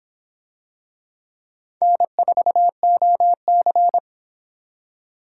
Morse code, heard as a keyed tone, about 24 wpm, 700 Hz. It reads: N4OC